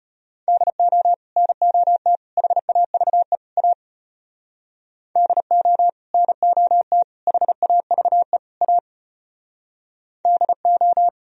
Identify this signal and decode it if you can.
Morse code, heard as a keyed tone, about 32 wpm, 700 Hz. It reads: DO NOT HAVE A DO NOT HAVE A DO